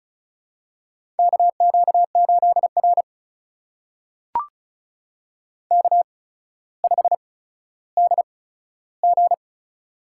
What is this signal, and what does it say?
Morse code, heard as a keyed tone, about 34 wpm, spaced as for 15 wpm, 700 Hz. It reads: KQ8R K5DG
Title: USA Call Signs - 4 Characters - 8x Character Spacing - 35wpm